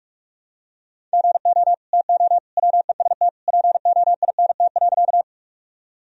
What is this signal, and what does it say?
Morse code, heard as a keyed tone, about 41 wpm, 700 Hz. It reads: GO TO WEST POINT.
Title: Sentences from Top 1000 Words - Review Part 2 - 45wpm